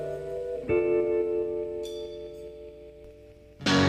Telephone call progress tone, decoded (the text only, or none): dial tone